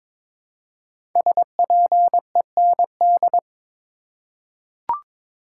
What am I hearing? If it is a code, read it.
Morse code, heard as a keyed tone, about 23 wpm, 700 Hz. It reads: SPEND
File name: 5 Letter Words - First 3 Letters Given - 22wpm